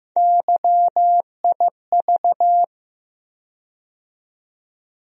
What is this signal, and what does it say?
Morse code, heard as a keyed tone, about 15 wpm, 700 Hz. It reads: YIV
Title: USA Call Sign Postfixes - 3 Letters 15wpm